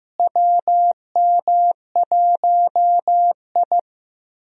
Morse code, heard as a keyed tone, about 16 wpm, 700 Hz. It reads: WM1I
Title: USA Call Signs - 4 Characters - Repeated 3x - 15wpm